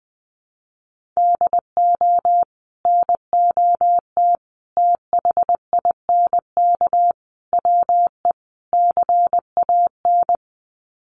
Morse code, heard as a keyed tone, about 20 wpm, 700 Hz. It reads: DO NOT THINK WE CAN
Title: Sets of 5 Words Encore2 20wpm